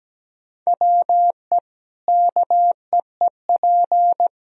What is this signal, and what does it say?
Morse code, heard as a keyed tone, about 17 wpm, 700 Hz. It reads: WE KEEP
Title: Sentences from Top 1000 Words - Review Part 4 - 17wpm